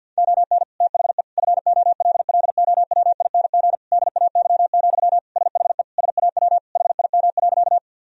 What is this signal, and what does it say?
Morse code, heard as a keyed tone, about 50 wpm, 700 Hz. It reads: ON THE FOLLOWING DAY, SHE SAW HIM.
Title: Sentences from Top 1000 Words - Review Part 4 - 50wpm